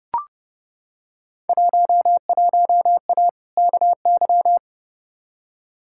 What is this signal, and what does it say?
Morse code, heard as a keyed tone, about 30 wpm, 700 Hz. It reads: 11A KY
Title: Field Day Exchanges Encore - 30wpm